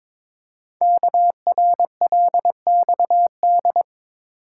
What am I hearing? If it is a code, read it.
Morse code, heard as a keyed tone, about 22 wpm, 700 Hz. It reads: KRLXD
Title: Code Groups - 5 Letters - 22wpm